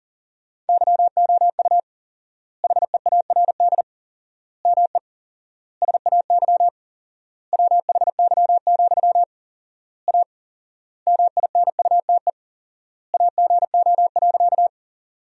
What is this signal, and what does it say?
Morse code, heard as a keyed tone, about 40 wpm, 700 Hz. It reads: YOU HEARD ME SAY WHY, A MINUTE AGO.
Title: 4x Word-Spacing - Sentences from Top 500 Words - 40wpm